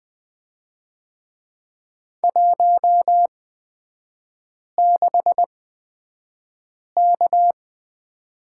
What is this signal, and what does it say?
Morse code, heard as a keyed tone, about 20 wpm, spaced as for 5 wpm, 690 Hz. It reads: 16K
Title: 229 Lesson 37e - Letter x - Three Characters 20wpm